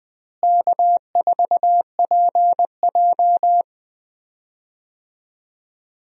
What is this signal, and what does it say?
Morse code, heard as a keyed tone, about 20 wpm, 700 Hz. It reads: K4PJ
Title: USA Call Signs - 4 Characters - Repeated 3x - 20wpm